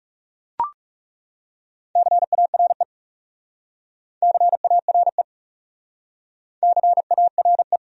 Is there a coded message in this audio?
Yes: Morse code from 2.0 to 7.8 s, heard as a keyed tone, about 38 wpm, 700 Hz. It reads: CARE CARE CARE